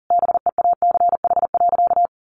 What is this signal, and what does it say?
Morse code, heard as a keyed tone, about 40 wpm, 700 Hz. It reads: BEACH.
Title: Sentences from Top 1500 Words 40wpm